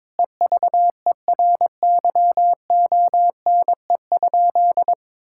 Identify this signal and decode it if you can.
Morse code, heard as a keyed tone, about 21 wpm, 700 Hz. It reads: EVERYONE?
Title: Sentences from Top 1000 Words - Review Part 5 - 22wpm